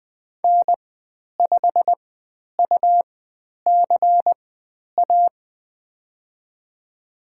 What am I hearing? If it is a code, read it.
Morse code, heard as a keyed tone, about 20 wpm, spaced as for 10 wpm, 700 Hz. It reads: N5UCA
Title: 102 Lesson 17g - Letter u - Call Signs 20wpm